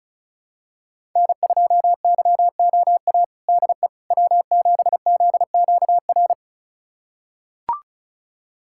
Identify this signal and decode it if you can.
Morse code, heard as a keyed tone, about 36 wpm, 700 Hz. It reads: N2YOA DE W7ZQR
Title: USA Callsign DE Callsign - 5 Characters - 35wpm